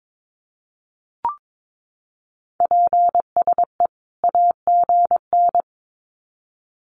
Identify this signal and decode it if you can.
Morse code, heard as a keyed tone, about 22 wpm, 700 Hz. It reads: PSE AGN
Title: Common QSO Phrases 22wpm